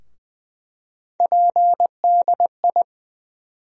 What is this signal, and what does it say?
Morse code, heard as a keyed tone, about 20 wpm, 700 Hz. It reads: PDI
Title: USA Call Sign Postfixes - 3 Letters 20wpm